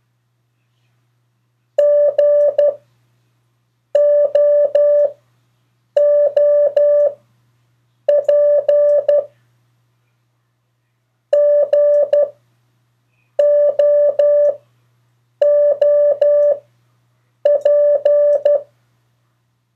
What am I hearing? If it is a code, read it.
Morse code, heard as a keyed tone, about 12 wpm, 580 Hz. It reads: G O O P G O O P
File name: 3of4-words-2